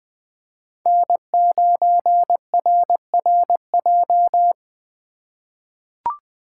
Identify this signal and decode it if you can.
Morse code, heard as a keyed tone, about 20 wpm, 700 Hz. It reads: N9RRJ
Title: USA Call Signs - 5 Characters - 4x Character Spacing - 20wpm